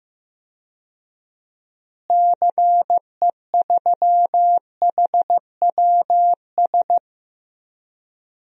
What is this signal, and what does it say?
Morse code, heard as a keyed tone, about 15 wpm, 700 Hz. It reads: CE3HWS
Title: DXCC-100 Call Signs - 6 Characters - Repeated 3x - 15wpm